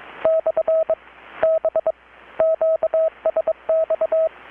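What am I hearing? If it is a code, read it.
Morse code, heard as a keyed tone, about 22 wpm, 640 Hz, with radio noise. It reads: / B QSX
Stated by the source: Morse from KPH's Night of Nights XX maritime radiotelegraphy event, received in Chiapas, Mexico about 3,800 km away on a Kenwood R-600 with a grounded 10 m long-wire antenna.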